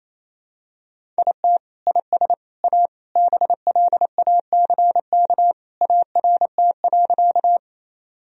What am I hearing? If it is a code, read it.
Morse code, heard as a keyed tone, about 27 wpm, 700 Hz. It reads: IT IS A BLACK ART.
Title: Sentences from Top 1000 Words - Review Part 3 - 28wpm